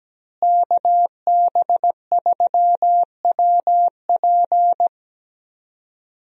Morse code, heard as a keyed tone, about 17 wpm, 700 Hz. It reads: KB3WP
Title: USA Call Signs - 5 Characters - Repeated 2x - 17wpm